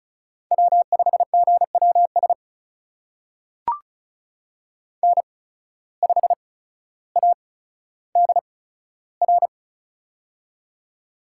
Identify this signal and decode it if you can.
Morse code, heard as a keyed tone, about 36 wpm, spaced as for 15 wpm, 700 Hz. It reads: W5GWS N5ADR
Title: USA Call Signs - 5 Characters - 8x Character Spacing - 35wpm